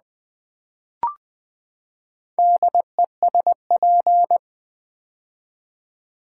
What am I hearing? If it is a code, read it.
Morse code, heard as a keyed tone, about 20 wpm, 700 Hz. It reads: DESP
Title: Build A Word - 4 Letters At A Time - 11-Letter Words - 20wpm